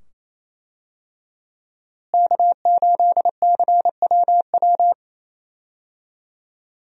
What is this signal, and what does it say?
Morse code, heard as a keyed tone, about 27 wpm, 700 Hz. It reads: K8CWW
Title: USA Call Signs - 5 Characters - Repeated 2x - 28wpm